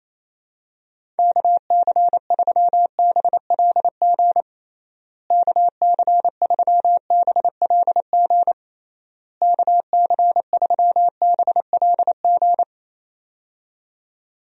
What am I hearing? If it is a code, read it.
Morse code, heard as a keyed tone, about 27 wpm, 700 Hz. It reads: KC3BLG KC3BLG KC3BLG
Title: USA Call Signs - 6 Characters - Repeated 3x - 28wpm